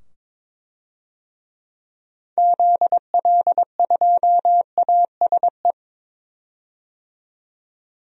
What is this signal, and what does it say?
Morse code, heard as a keyed tone, about 22 wpm, 700 Hz. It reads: ZL2ASE
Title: DXCC-100 Call Signs - 6 Characters - 22wpm